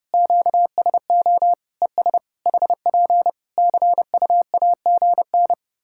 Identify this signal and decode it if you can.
Morse code, heard as a keyed tone, about 32 wpm, 700 Hz. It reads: QSO ES HP CUAGN
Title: Practice QSO - Exchange 4 - Ending - 30wpm